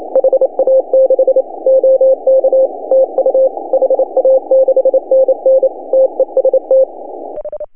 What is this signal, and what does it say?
Morse code, heard as a keyed tone, about 27 wpm, 530 Hz. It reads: HA6 OK TU HA6NN TEST
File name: CW / HA2OS RHOB 2017 CW